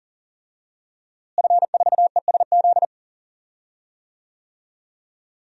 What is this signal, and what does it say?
Morse code, heard as a keyed tone, about 40 wpm, 700 Hz. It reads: F4ESZ
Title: DXCC-100 Call Signs - 5 Characters - Repeated 2x - 40wpm